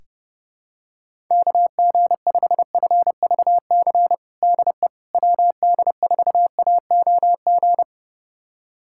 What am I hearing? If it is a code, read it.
Morse code, heard as a keyed tone, about 30 wpm, 700 Hz. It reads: KG5FVC DE WD4AOG
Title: USA Callsign DE Callsign 30wpm